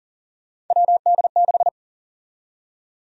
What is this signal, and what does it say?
Morse code, heard as a keyed tone, about 40 wpm, 700 Hz. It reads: WD6
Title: USA Call Sign Prefixes - 40wpm